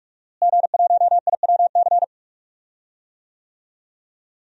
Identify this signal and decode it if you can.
Morse code, heard as a keyed tone, about 46 wpm, 700 Hz. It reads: G1IWC
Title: DXCC-100 Call Signs - 5 Characters - Repeated 3x - 45wpm